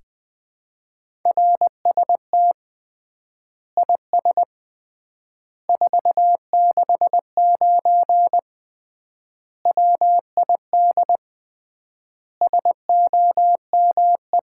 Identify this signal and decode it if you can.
Morse code, heard as a keyed tone, about 20 wpm, 700 Hz. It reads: RST IS 469 WID SOME
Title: Common QSO Phrases - 3x Word-Spacing - 20wpm